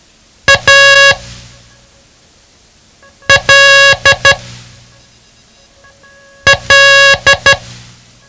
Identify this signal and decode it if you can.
Morse code, heard as a keyed tone, about 12 wpm, spaced as for 4 wpm, 1650 Hz. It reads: ALL
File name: ham radio / CodeQuick / Code Quick 07